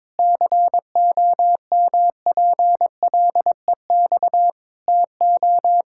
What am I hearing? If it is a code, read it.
Morse code, heard as a keyed tone, about 21 wpm, 700 Hz. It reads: COMPLEX TO
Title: Sentences from Top 1000 Words - Review Part 6 - 22wpm